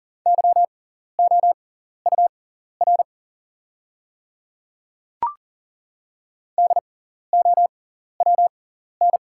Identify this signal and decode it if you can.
Morse code, heard as a keyed tone, about 40 wpm, spaced as for 14 wpm, 700 Hz. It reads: YOUR DOWN
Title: 6x Character Spacing - Top 100 Words - 40wpm